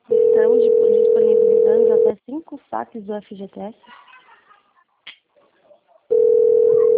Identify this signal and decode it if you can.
Ringback tone (North America), 440 + 480 Hz, with speech.
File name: out- - - - - .92997